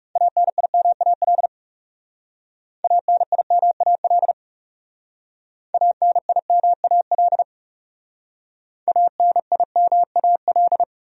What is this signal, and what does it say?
Morse code, heard as a keyed tone, about 37 wpm, 700 Hz. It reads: ANIMAL ANIMAL ANIMAL ANIMAL
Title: Speed-Racing Top 200 Words 30wpm